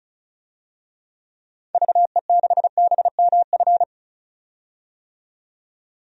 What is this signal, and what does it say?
Morse code, heard as a keyed tone, about 35 wpm, 700 Hz. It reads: VE6BMF
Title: DXCC-100 Call Signs - 6 Characters - 35wpm